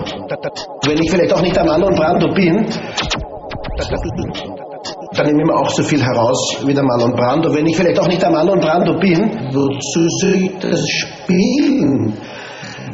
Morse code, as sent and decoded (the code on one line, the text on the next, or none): none